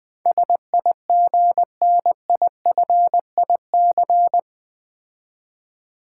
Morse code, heard as a keyed tone, about 20 wpm, 700 Hz. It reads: SIGNIFIC
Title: Build A Word - 4 Letters At A Time - 12-Letter Words - 20wpm